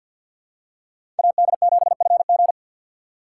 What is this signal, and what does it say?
Morse code, heard as a keyed tone, about 49 wpm, 700 Hz. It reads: AD7FG
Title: USA Call Signs - 2x Character Spacing - 50wpm